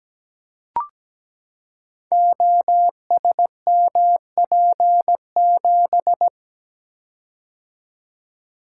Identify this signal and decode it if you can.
Morse code, heard as a keyed tone, about 17 wpm, 700 Hz. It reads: OSMP7
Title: Code Groups - 5 Letter:Numbers - 17wpm